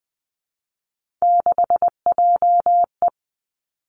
Morse code, heard as a keyed tone, about 20 wpm, 700 Hz. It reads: 6JE
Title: Three Challenging Letter-Number Combinations 20wpm